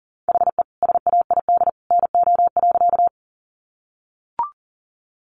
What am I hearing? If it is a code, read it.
Morse code, heard as a keyed tone, about 40 wpm, 700 Hz. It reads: HE SAID NO.
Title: Sentences from Top 100 Words 40wpm